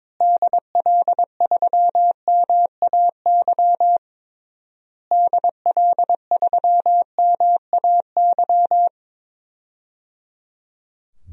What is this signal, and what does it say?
Morse code, heard as a keyed tone, about 22 wpm, 700 Hz. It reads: DL3MAY DL3MAY
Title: DXCC-100 Call Signs - 6 Characters - Repeated 3x - 22wpm